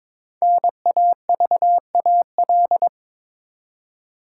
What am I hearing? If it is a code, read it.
Morse code, heard as a keyed tone, about 22 wpm, 700 Hz. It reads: NAVAL
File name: 5 Letter Words - First 3 Letters Given - 22wpm